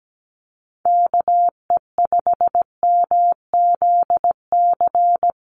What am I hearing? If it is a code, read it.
Morse code, heard as a keyed tone, about 17 wpm, 700 Hz. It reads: KE5MZC
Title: USA Call Signs 17wpm